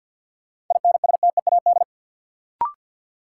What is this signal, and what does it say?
Morse code, heard as a keyed tone, about 47 wpm, 700 Hz. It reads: INSTEAD
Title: Sentences from Top 800 Words - New Word Drill - 50wpm